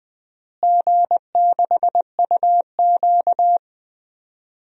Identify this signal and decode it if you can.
Morse code, heard as a keyed tone, about 20 wpm, 700 Hz. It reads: G6UQ
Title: DXCC-100 Call Signs - 4 Characters - Repeated 3x - 20wpm